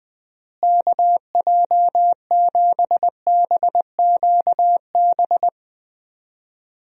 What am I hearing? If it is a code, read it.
Morse code, heard as a keyed tone, about 20 wpm, 700 Hz. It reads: KJ7BQB
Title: USA Call Signs - 6 Characters - Repeated 3x - 20wpm